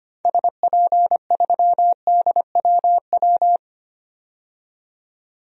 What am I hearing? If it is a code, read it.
Morse code, heard as a keyed tone, about 25 wpm, 700 Hz. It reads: SP3DWW
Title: DXCC-100 Call Signs - 6 Characters - 25wpm